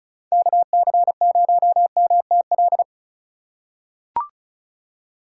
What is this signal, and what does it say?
Morse code, heard as a keyed tone, about 34 wpm, 700 Hz. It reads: KC0MTL
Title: USA Call Signs - 6 Characters - 4x Character Spacing - 35wpm